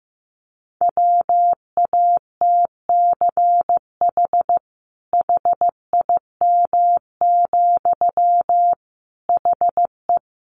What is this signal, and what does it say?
Morse code, heard as a keyed tone, about 15 wpm, 700 Hz. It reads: WATCH HIM, HE
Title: Sentences from Top 300 Words 15wpm